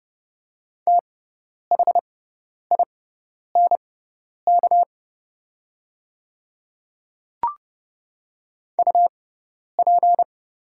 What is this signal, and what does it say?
Morse code, heard as a keyed tone, about 30 wpm, spaced as for 10 wpm, 700 Hz. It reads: THINK UP
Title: 6x Character Spacing - Top 200 Words - 30wpm